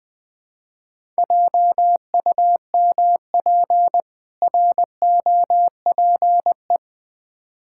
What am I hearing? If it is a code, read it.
Morse code, heard as a keyed tone, about 20 wpm, 700 Hz. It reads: JUMP ROPE
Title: Difficult Words with Hints - 20wpm